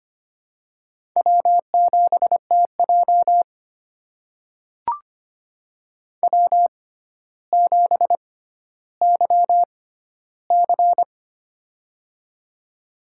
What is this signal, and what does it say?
Morse code, heard as a keyed tone, about 25 wpm, spaced as for 13 wpm, 700 Hz. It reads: W7TJ W7YC
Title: USA Call Signs - 4 Characters - 6x Character Spacing - 25wpm